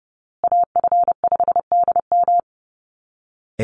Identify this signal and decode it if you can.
Morse code, heard as a keyed tone, about 30 wpm, 700 Hz. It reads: AF5DM
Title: Call Signs Repeated 2x - 30wpm